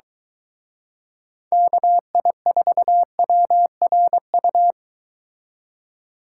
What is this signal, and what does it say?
Morse code, heard as a keyed tone, about 23 wpm, 700 Hz. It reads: KI4WRU